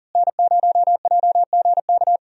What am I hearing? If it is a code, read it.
Morse code, heard as a keyed tone, about 40 wpm, 700 Hz. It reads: N0JGK
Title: USA Call Signs - 5 Characters - Repeated 2x - 40wpm